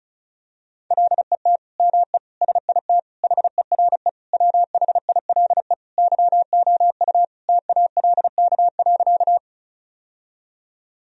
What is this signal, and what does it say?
Morse code, heard as a keyed tone, about 34 wpm, 700 Hz. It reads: LET ME SIT HERE WHILE YOU TALK.
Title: Sentences from Top 1000 Words - Review Part 1 - 35wpm